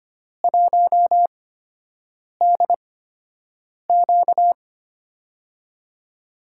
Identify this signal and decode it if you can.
Morse code, heard as a keyed tone, about 25 wpm, spaced as for 7 wpm, 700 Hz. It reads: 1DQ
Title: USA Call Signs - 4 Characters - 8x Character Spacing - 25wpm